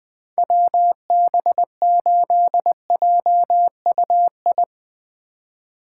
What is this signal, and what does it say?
Morse code, heard as a keyed tone, about 20 wpm, 700 Hz. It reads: WB8JUI
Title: K1USN SST QSO Practice - 20wpm